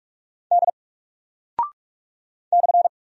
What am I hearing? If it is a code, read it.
Morse code, heard as a keyed tone, about 44 wpm, 700 Hz. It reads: D /
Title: Single Character (Letter:Number:Punctuation:Prosign) 45wpm